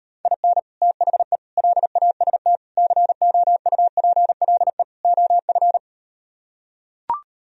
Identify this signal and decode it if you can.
Morse code, heard as a keyed tone, about 38 wpm, 700 Hz. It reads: IN THE LAST COUPLE OF